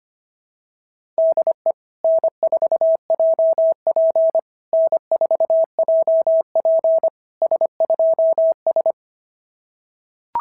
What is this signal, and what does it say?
Morse code, heard as a keyed tone, about 25 wpm, 660 Hz. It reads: DE N4JP N4JP S2S
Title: SOTA QSO Practice 25wpm